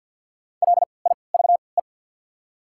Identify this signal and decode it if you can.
Morse code, heard as a keyed tone, about 53 wpm, spaced as for 29 wpm, 690 Hz. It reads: LIVE
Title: 3x Character Spacing - Top 300 Words - 50wpm